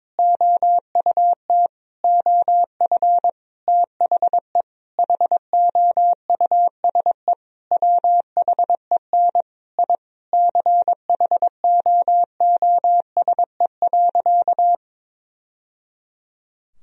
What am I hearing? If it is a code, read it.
Morse code, heard as a keyed tone, about 22 wpm, 700 Hz. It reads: OUT OF THE HOUSE WHEN I CHOOSE.
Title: Sentences from Top 1000 Words - Review Part 4 - 22wpm